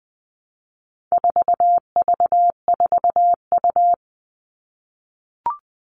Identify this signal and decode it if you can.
Morse code, heard as a keyed tone, about 20 wpm, 700 Hz. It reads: 4V4U
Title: All Cut Numbers - 4-Digit 20wpm